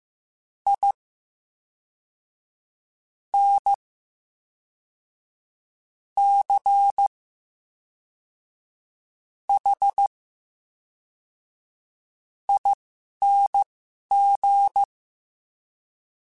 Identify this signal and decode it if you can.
Morse code, heard as a keyed tone, about 15 wpm, spaced as for 4 wpm, 800 Hz. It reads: INCHING
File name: Special INGs / Special INGs-mp3s / VerySlowToFaster Ing @10